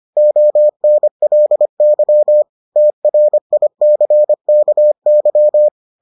Morse code, heard as a keyed tone, about 25 wpm, 600 Hz. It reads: ONLY TRICKY